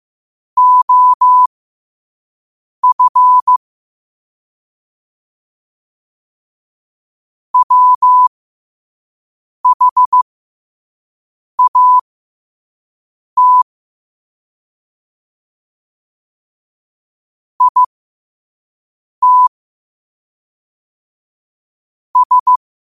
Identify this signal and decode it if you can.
Morse code, heard as a keyed tone, about 15 wpm, spaced as for 5 wpm, 1000 Hz. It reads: OF WHAT IT S
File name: qotdfiles / 5wpm